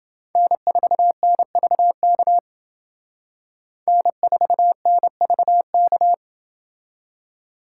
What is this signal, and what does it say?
Morse code, heard as a keyed tone, about 29 wpm, 700 Hz. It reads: N4NVK N4NVK